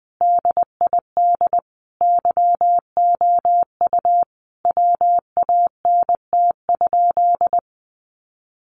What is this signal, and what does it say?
Morse code, heard as a keyed tone, about 20 wpm, 700 Hz. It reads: DID YOU WANT?
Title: Sentences from Top 200 Words 20wpm